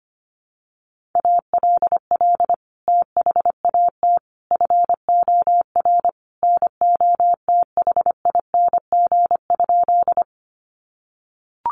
Morse code, heard as a keyed tone, about 25 wpm, 700 Hz. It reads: ALL THAT FOR NOTHING?
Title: Sentences from Top 500 Words 25wpm